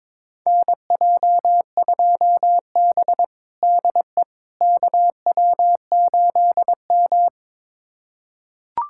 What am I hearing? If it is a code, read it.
Morse code, heard as a keyed tone, about 22 wpm, 700 Hz. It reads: NJ2B DE KW8M
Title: USA Callsign DE Callsign - 4 Characters - 22wpm